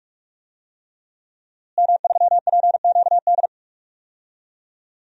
Morse code, heard as a keyed tone, about 43 wpm, 700 Hz. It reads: M3PQD
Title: DXCC-100 Call Signs - 5 Characters - Repeated 3x - 45wpm